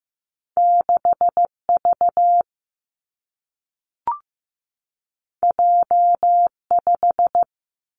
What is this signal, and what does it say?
Morse code, heard as a keyed tone, about 15 wpm, 700 Hz. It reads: 6 V J 5
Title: Two Challenging Letter-Number Combinations 15wpm